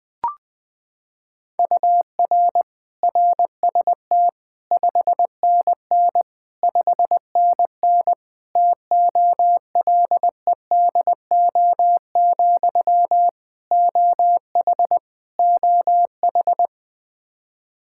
Morse code, heard as a keyed tone, about 20 wpm, 700 Hz. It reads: UR RST 5NN 5NN TOLEDO, OH OH
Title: Practice QSO - Exchange 1 - Beginning - 20wpm